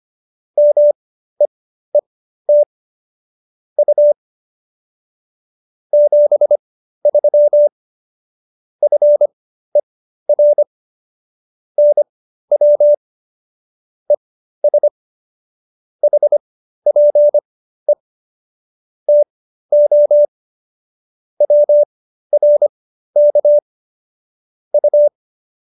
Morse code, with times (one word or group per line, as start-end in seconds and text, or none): 0.6-2.6 MEET
3.8-4.1 U
5.9-7.7 73
8.8-10.6 FER
11.8-12.9 NW
14.1-14.9 ES
16.0-17.9 HPE
19.1-20.3 TO
21.4-23.6 WRK
24.7-25.1 U